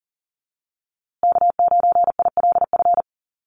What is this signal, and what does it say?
Morse code, heard as a keyed tone, about 40 wpm, 700 Hz. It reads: K9ILF
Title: USA Call Signs Repeated 3x 40wpm